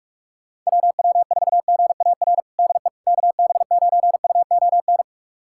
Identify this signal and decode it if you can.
Morse code, heard as a keyed tone, about 46 wpm, 700 Hz. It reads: WW4GAR DE KB9UON
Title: USA Callsign DE Callsign - 6 Characters - 45wpm